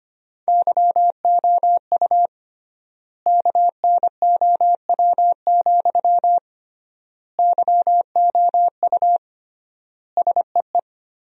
Morse code, heard as a keyed tone, about 25 wpm, 700 Hz. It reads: YOU KNOW, YOU SEE
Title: 3x Word-Spacing - Sentences from Top 200 Words - 25wpm